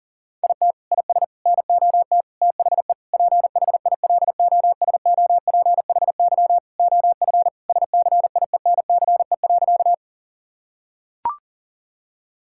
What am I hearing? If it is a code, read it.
Morse code, heard as a keyed tone, about 40 wpm, 700 Hz. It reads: IT IS NOT THE PHILOSOPHY OF SCIENCE.
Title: Sentences from Top 1000 Words - Review Part 6 - 40wpm